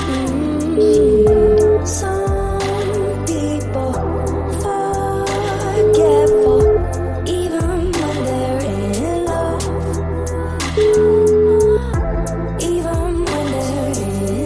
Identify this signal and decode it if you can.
Ringback tone (Europe, 425 Hz), with noise.